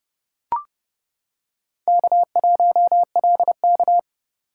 Morse code, heard as a keyed tone, about 30 wpm, 700 Hz. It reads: K1LK